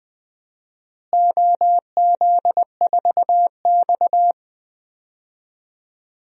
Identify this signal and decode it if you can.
Morse code, heard as a keyed tone, about 20 wpm, 700 Hz. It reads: OZ4X
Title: DXCC-100 Call Signs - 4 Characters - Repeated 3x - 20wpm